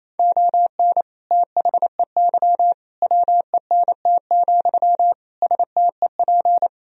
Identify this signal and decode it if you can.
Morse code, heard as a keyed tone, about 28 wpm, 700 Hz. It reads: ON THEY WENT, STEP
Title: Sentences from Top 1000 Words - Review Part 2 - 28wpm